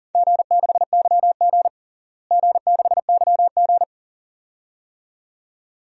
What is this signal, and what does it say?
Morse code, heard as a keyed tone, about 40 wpm, 700 Hz. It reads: G6YG G6YG
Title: DXCC-100 Call Signs - 4 Characters - Repeated 2x - 40wpm